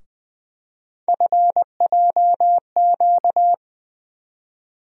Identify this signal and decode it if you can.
Morse code, heard as a keyed tone, about 20 wpm, 700 Hz. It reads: FJQ
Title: USA Call Sign Postfixes - 3 Letters 20wpm